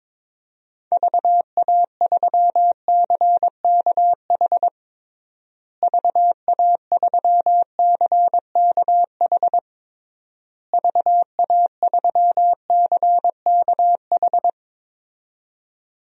Morse code, heard as a keyed tone, about 22 wpm, 700 Hz. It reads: VA3CKH VA3CKH VA3CKH
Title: DXCC-100 Call Signs - 6 Characters - Repeated 3x - 22wpm